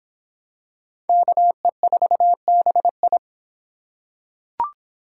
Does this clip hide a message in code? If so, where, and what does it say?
Yes: Morse code from 1.1 to 3.2 s, heard as a keyed tone, about 26 wpm, 700 Hz. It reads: KE4BI